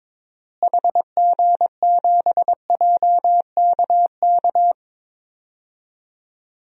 Morse code, heard as a keyed tone, about 22 wpm, 700 Hz. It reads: HG7JKK
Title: DXCC-100 Call Signs - 6 Characters - Repeated 3x - 22wpm